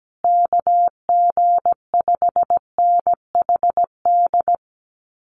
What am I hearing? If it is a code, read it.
Morse code, heard as a keyed tone, about 17 wpm, 700 Hz. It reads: KG5NHD
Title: Common QSO Phrases Encore 17wpm